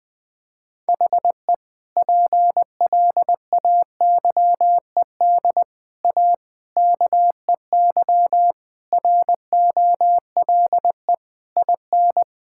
Morse code, heard as a keyed tone, about 20 wpm, 700 Hz. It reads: HE PLAYED A KEY ROLE IN
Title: Sentences with 6-Word Phrases - Spoken Beginning - 20wpm